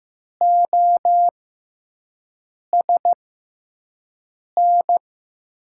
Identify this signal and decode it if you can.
Morse code, heard as a keyed tone, about 15 wpm, spaced as for 5 wpm, 700 Hz. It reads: OSN
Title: USA Call Signs - 5 Characters - 6x Character Spacing - 15wpm